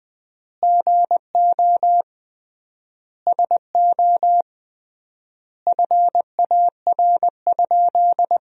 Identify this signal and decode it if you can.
Morse code, heard as a keyed tone, about 20 wpm, 700 Hz. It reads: GO SO FAR?
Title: 3x Word-Spacing - Sentences from Top 300 Words - 20wpm